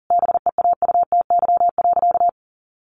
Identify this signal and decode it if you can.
Morse code, heard as a keyed tone, about 40 wpm, 700 Hz. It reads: BEAUTY.
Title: Sentences from Top 1200 Words 40wpm